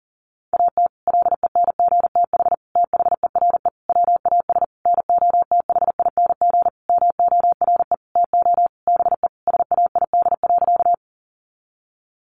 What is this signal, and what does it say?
Morse code, heard as a keyed tone, about 40 wpm, 700 Hz. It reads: AT LENGTH THERE WAS NOTHING MORE TO BE SAID.
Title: Sentences from Top 1000 Words 40wpm